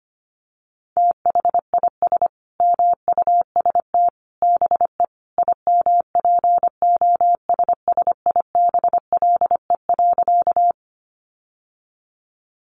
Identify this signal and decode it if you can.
Morse code, heard as a keyed tone, about 25 wpm, 700 Hz. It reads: THIS MUST BE IMPOSSIBLE.
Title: Sentences from Top 1100 Words 25wpm